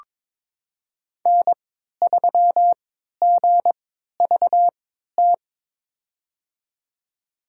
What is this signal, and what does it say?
Morse code, heard as a keyed tone, about 22 wpm, spaced as for 12 wpm, 700 Hz. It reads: N3GVT
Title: USA Call Signs - 5 Characters - 3x Character Spacing - 22wpm